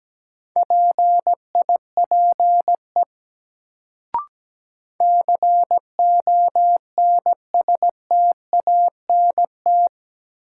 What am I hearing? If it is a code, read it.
Morse code, heard as a keyed tone, about 17 wpm, 700 Hz. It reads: PIPE CONSTANT
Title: Sentences from Top 1500 Words - New Word Drill - 17wpm